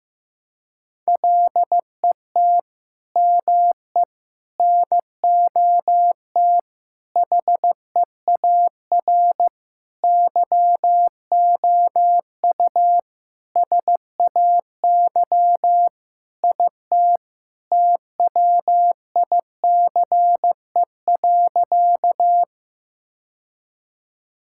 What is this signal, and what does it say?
Morse code, heard as a keyed tone, about 15 wpm, 700 Hz. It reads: LET ME NOT HEAR YOU SAY IT TWICE.
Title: Sentences from Top 1600 Words 15wpm